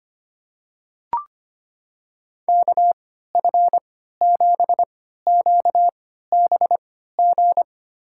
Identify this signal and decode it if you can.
Morse code, heard as a keyed tone, about 25 wpm, spaced as for 14 wpm, 700 Hz. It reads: KF7QBG
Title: USA Call Signs - 6 Characters - 3x Character Spacing - 25wpm